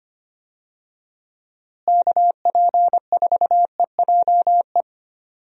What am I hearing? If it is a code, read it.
Morse code, heard as a keyed tone, about 25 wpm, 700 Hz. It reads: KP4EJE
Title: DXCC-100 Call Signs - 6 Characters - Repeated 3x - 25wpm